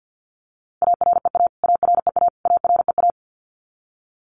Morse code, heard as a keyed tone, about 50 wpm, 700 Hz. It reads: AREA AREA AREA
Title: Top 500 Words Repeated 50wpm